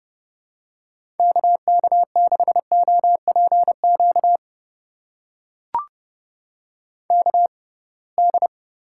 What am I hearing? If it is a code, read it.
Morse code, heard as a keyed tone, about 32 wpm, spaced as for 23 wpm, 700 Hz. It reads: KK6OPQ KD
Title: USA Call Signs - 6 Characters - 6x Character Spacing - 30wpm